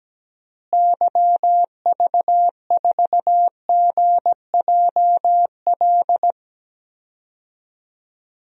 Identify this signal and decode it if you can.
Morse code, heard as a keyed tone, about 17 wpm, 700 Hz. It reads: YV4GJL
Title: DXCC-100 Call Signs - 6 Characters - Repeated 2x - 17wpm